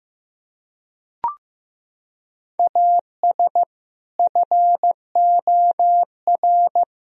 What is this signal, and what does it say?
Morse code, heard as a keyed tone, about 15 wpm, 700 Hz. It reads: AS FOR
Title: Sentences from Top 1000 Words - Review Part 3 - 15wpm